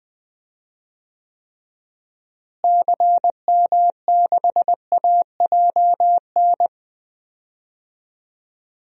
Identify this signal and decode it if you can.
Morse code, heard as a keyed tone, about 20 wpm, 700 Hz. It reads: CM6AJN
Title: DXCC-100 Call Signs - 6 Characters - Repeated 3x - 20wpm